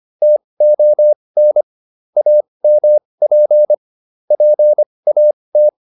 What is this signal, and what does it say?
Morse code, heard as a keyed tone, about 25 wpm, spaced as for 20 wpm, 600 Hz. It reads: TON AMP PAT